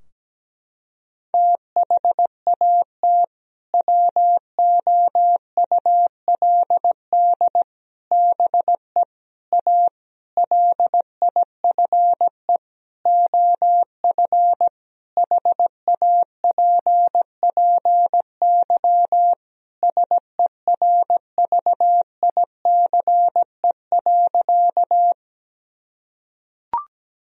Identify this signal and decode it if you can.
Morse code, heard as a keyed tone, about 17 wpm, 700 Hz. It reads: THAT WOULD BE A LIFE OF HAPPY SERVICE.
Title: Sentences from Top 1000 Words - Review Part 4 - 17wpm